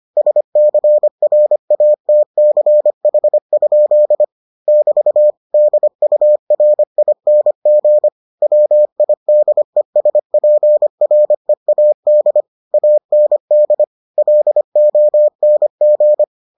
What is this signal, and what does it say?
Morse code, heard as a keyed tone, about 25 wpm, 600 Hz. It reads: SCRATCH? <BT> DURING WIDESPREAD AND LONG